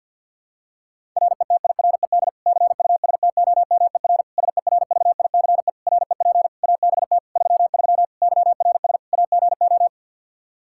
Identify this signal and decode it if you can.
Morse code, heard as a keyed tone, about 51 wpm, 700 Hz. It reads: RETIRED CUSTOMER SERVICE REP ABT 33 YRS AGO
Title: Practice QSO - Exchange 3 - More Info - 50wpm